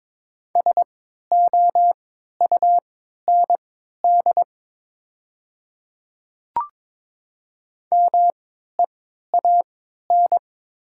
Morse code, heard as a keyed tone, about 22 wpm, spaced as for 13 wpm, 700 Hz. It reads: SOUND MEAN
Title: 3x Character Spacing - Top 300 Words - 22wpm